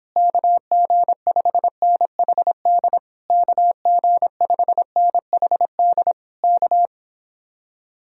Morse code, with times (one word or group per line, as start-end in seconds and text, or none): 0.2-3.0 KG5NHD
3.3-6.1 KG5NHD
6.4-6.9 K